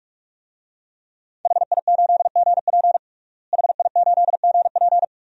Morse code, heard as a keyed tone, about 42 wpm, 700 Hz. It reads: HI8GP HI8GP
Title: DXCC-100 Call Signs - 5 Characters - Repeated 2x - 45wpm